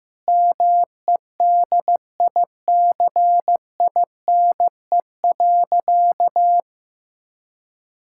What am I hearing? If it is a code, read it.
Morse code, heard as a keyed tone, about 15 wpm, 700 Hz. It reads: MEDICINE.
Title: Sentences from Top 1000 Words - Review Part 6 - 15wpm